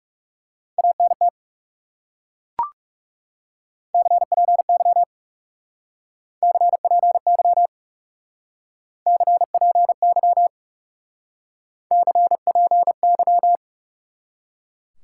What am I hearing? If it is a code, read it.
Morse code, heard as a keyed tone, about 37 wpm, 700 Hz. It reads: ANT CPY CPY CPY CPY
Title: Speed-Racing Top 180 QSO Elements 30wpm